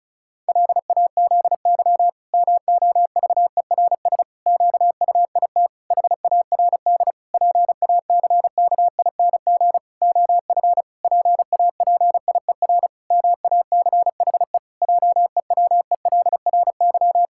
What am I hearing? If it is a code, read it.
Morse code, heard as a keyed tone, about 36 wpm, 700 Hz. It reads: LAZY MOVERS QUIT HARD PACKING OF PAPIER MACHE JEWELRY
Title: Pangram Challenge v2 - 35wpm